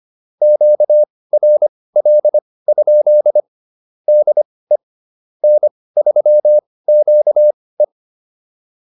Morse code, heard as a keyed tone, about 25 wpm, spaced as for 18 wpm, 600 Hz. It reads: QRL? DE N3QE